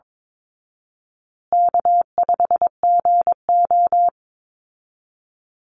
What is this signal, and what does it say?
Morse code, heard as a keyed tone, about 22 wpm, 700 Hz. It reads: K5GO
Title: Easy Call Signs Repeated 3x - 22wpm